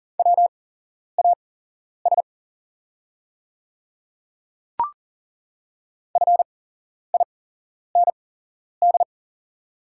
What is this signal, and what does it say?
Morse code, heard as a keyed tone, about 40 wpm, spaced as for 11 wpm, 690 Hz. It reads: WAS FIND
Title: 8x Character Spacing - Top 100 Words - 40wpm